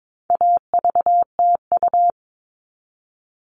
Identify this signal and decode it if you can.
Morse code, heard as a keyed tone, about 22 wpm, 700 Hz. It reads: AVTU
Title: All Cut Numbers - 4-Digit 22wpm